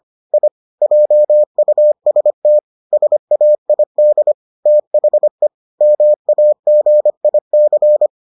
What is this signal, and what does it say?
Morse code, heard as a keyed tone, about 25 wpm, 600 Hz. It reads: I JUST SAID THE MAGIC